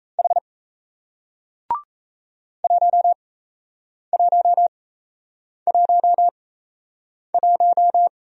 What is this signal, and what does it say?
Morse code, heard as a keyed tone, about 37 wpm, spaced as for 8 wpm, 700 Hz. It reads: H 1111